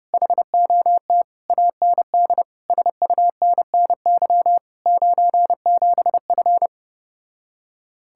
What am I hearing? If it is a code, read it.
Morse code, heard as a keyed tone, about 33 wpm, 700 Hz. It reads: HOT AND SUNNY 97F